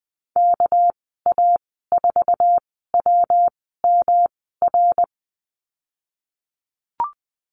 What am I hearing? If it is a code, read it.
Morse code, heard as a keyed tone, about 20 wpm, spaced as for 15 wpm, 700 Hz. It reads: KA4WMR
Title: 231 Lesson 37g - Letter x - Call Signs 20wpm